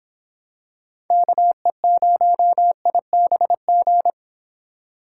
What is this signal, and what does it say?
Morse code, heard as a keyed tone, about 26 wpm, 700 Hz. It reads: KE0IBG